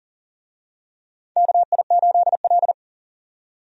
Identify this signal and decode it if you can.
Morse code, heard as a keyed tone, about 40 wpm, 700 Hz. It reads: KI8L
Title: USA Call Signs - 4 Characters - 40wpm